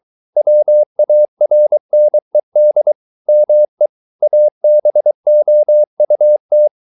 Morse code, heard as a keyed tone, about 23 wpm, 600 Hz. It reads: WARNED ME ABOUT